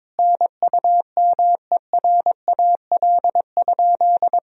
Morse code, heard as a keyed tone, about 22 wpm, 700 Hz. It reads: NUMERAL?
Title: Sentences from Top 1000 Words - Review Part 2 - 22wpm